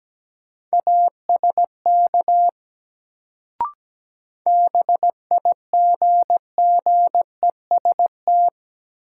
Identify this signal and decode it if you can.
Morse code, heard as a keyed tone, about 17 wpm, 700 Hz. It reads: ASK BIGGEST
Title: Sets of 4 Words - New Word Drill - 17wpm